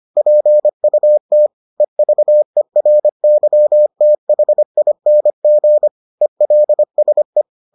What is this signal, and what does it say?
Morse code, heard as a keyed tone, about 25 wpm, 600 Hz. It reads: PUT EVERYTHING ELSE